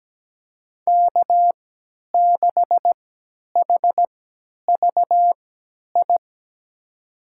Morse code, heard as a keyed tone, about 17 wpm, spaced as for 10 wpm, 700 Hz. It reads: K6HVI
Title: USA Call Signs - 5 Characters - 3x Character Spacing - 17wpm